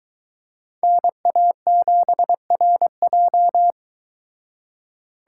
Morse code, heard as a keyed tone, about 23 wpm, 700 Hz. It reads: NA7RJ